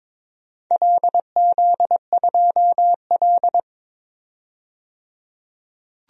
Morse code, heard as a keyed tone, about 23 wpm, 700 Hz. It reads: LZ2L
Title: DXCC-100 Call Signs - 4 Characters - 22wpm